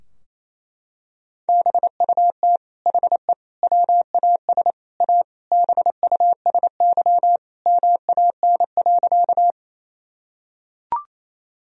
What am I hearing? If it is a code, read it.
Morse code, heard as a keyed tone, about 27 wpm, 690 Hz. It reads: BUT HE WAS A BUSY MAN.
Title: Sentences from Top 1000 Words - Review Part 3 - 28wpm